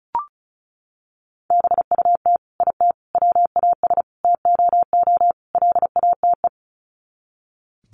Morse code, heard as a keyed tone, about 34 wpm, 700 Hz. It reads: BUT IT WAS TOO LATE
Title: Sets of 5 Words Encore2 35wpm